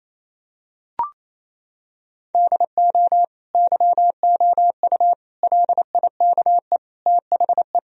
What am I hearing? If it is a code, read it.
Morse code, heard as a keyed tone, about 28 wpm, 700 Hz. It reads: DO YOU LIKE THE
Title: Sentences from Top 1000 Words - Review Part 2 - 28wpm